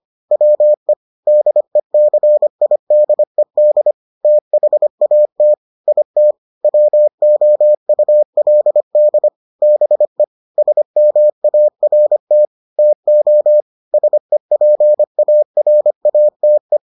Morse code, heard as a keyed tone, about 25 wpm, 600 Hz. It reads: WE DECIDED THAT IT WOULD BE SMART TO SEPARATE